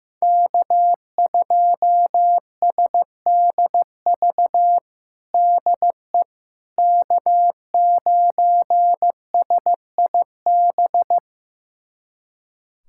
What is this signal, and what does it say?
Morse code, heard as a keyed tone, about 15 wpm, 700 Hz. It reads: K2SDV DE K9SIB
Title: USA Callsign DE Callsign - 5 Characters - 15wpm